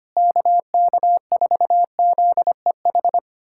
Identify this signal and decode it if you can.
Morse code, heard as a keyed tone, about 25 wpm, 700 Hz. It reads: KK4ZEH
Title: SOTA QSO Practice 25wpm